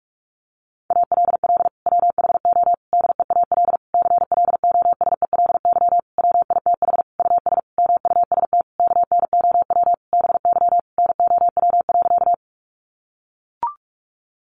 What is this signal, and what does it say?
Morse code, heard as a keyed tone, about 44 wpm, 700 Hz. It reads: ALL WHO DEAL CLOSELY WITH US MUST KNOW BY NOW.
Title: Sentences from Top 1400 Words 45wpm